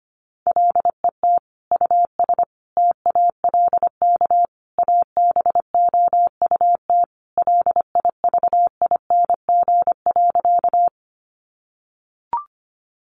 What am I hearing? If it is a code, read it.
Morse code, heard as a keyed tone, about 25 wpm, 700 Hz. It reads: LET US TALK ABOUT LIVING.
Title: Sentences from Top 1100 Words 25wpm